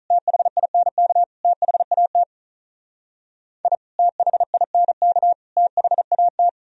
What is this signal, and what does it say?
Morse code, heard as a keyed tone, about 44 wpm, 700 Hz. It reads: THINK THAT I THINK THAT